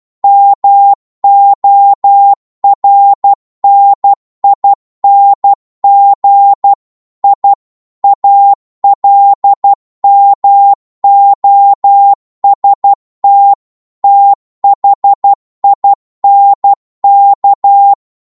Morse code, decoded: MORNING I ALMOST THINK